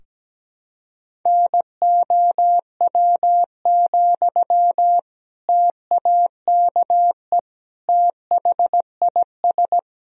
Morse code, heard as a keyed tone, about 17 wpm, 700 Hz. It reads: NOW, TAKE THIS
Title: Sentences from Top 400 Words and Abbreviations 17wpm